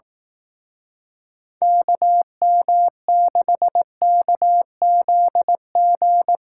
Morse code, heard as a keyed tone, about 18 wpm, 700 Hz. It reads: KM6KZG
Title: Speed Racing - USA 2x3 Call Signs 15wpm